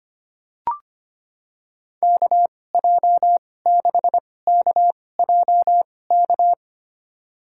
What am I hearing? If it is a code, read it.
Morse code, heard as a keyed tone, about 25 wpm, spaced as for 18 wpm, 700 Hz. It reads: KJ6KJK
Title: USA Call Signs - 2x Character Spacing - 25wpm